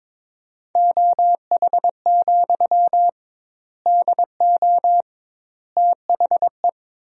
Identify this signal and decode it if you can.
Morse code, heard as a keyed tone, about 22 wpm, 700 Hz. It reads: OH, DO THE
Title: 2x Word-Spacing - Sentences from Top 500 Words - 22wpm